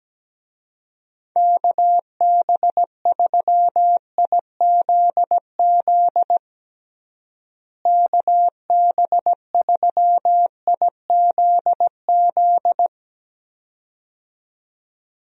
Morse code, heard as a keyed tone, about 17 wpm, 700 Hz. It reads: KB3IZZ KB3IZZ
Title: USA Call Signs - 6 Characters - Repeated 2x - 17wpm